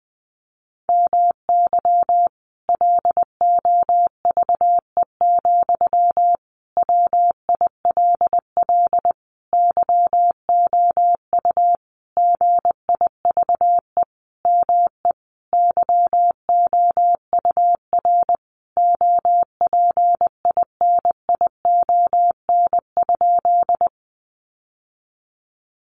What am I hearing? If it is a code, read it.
Morse code, heard as a keyed tone, about 20 wpm, 700 Hz. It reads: MY LOVE, WILL YOU GIVE ME YOUR OPINION?
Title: Sentences from Top 1200 Words 20wpm